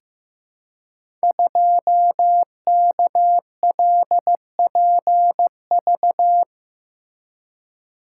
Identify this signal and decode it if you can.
Morse code, heard as a keyed tone, about 15 wpm, 700 Hz. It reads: 2KLPV
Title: Code Groups - 5 Letter:Numbers - 15wpm